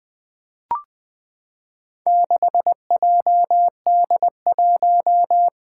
Morse code, heard as a keyed tone, about 20 wpm, 700 Hz. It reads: 6JD1
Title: Four Challenging Letter-Number Combinations 20wpm